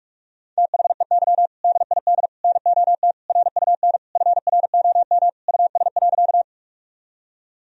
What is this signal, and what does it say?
Morse code, heard as a keyed tone, about 46 wpm, 700 Hz. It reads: THEY DID NOT RUN FROM US.
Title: Sentences from Top 1000 Words - Review Part 1 - 45wpm